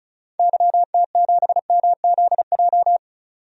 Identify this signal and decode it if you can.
Morse code, heard as a keyed tone, about 36 wpm, 700 Hz. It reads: YT7MZJ
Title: DXCC-100 Call Signs - 6 Characters - 35wpm